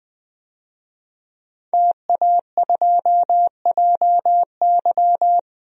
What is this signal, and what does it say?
Morse code, heard as a keyed tone, about 20 wpm, 700 Hz. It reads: TA2JY
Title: DXCC-100 Call Signs - 5 Characters - Repeated 2x - 20wpm